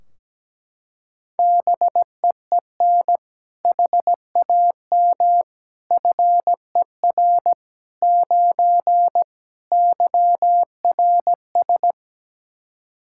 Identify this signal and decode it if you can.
Morse code, heard as a keyed tone, about 17 wpm, 700 Hz. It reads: BEEN HAM FER 9 YRS
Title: Practice QSO - Exchange 3 - More Info - 17wpm